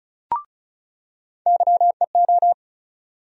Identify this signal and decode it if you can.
Morse code, heard as a keyed tone, about 34 wpm, 700 Hz. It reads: YEO
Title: USA Call Sign Postfixes - 3 Letters 35wpm